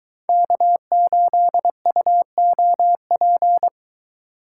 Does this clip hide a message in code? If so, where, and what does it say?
Yes: Morse code from 0.3 to 3.7 s, heard as a keyed tone, about 21 wpm, 700 Hz. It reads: K8UOP